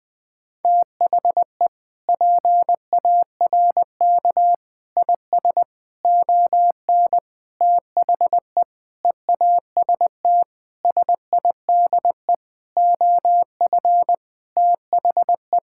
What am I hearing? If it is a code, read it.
Morse code, heard as a keyed tone, about 20 wpm, 700 Hz. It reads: THE PARK IS ON THE EAST SIDE OF THE
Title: Sentences with 6-Word Phrases - Spoken Beginning - 20wpm